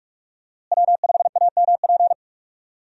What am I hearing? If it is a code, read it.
Morse code, heard as a keyed tone, about 46 wpm, 700 Hz. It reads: W5AMP
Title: USA Call Signs - 5 Characters - Repeated 2x - 45wpm